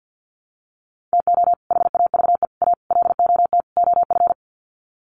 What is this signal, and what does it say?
Morse code, heard as a keyed tone, about 50 wpm, 700 Hz. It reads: TO HAVE A LOT OF
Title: Sets of 5 Words Encore 50wpm